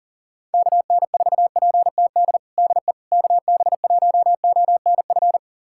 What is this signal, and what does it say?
Morse code, heard as a keyed tone, about 40 wpm, 700 Hz. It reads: KN4PTD DE KB1ONF
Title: USA Callsign DE Callsign - 6 Characters - 40wpm